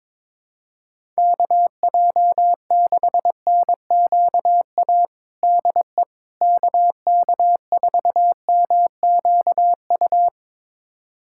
Morse code, heard as a keyed tone, about 22 wpm, 700 Hz. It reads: KJ6NQA DE KK4MQU
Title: USA Callsign DE Callsign 22wpm